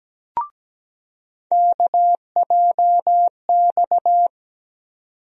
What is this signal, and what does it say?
Morse code, heard as a keyed tone, about 17 wpm, 700 Hz. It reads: KJX
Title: USA Call Sign Postfixes - 3 Letters 17wpm